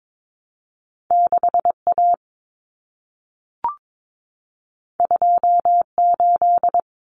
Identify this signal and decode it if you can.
Morse code, heard as a keyed tone, about 22 wpm, 700 Hz. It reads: 6A 28
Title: Common Cut Numbers - 2-Digit 22wpm